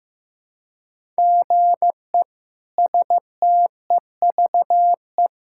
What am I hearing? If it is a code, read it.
Morse code, heard as a keyed tone, about 15 wpm, 700 Hz. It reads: GE STEVE
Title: K1USN SST QSO Practice - 15wpm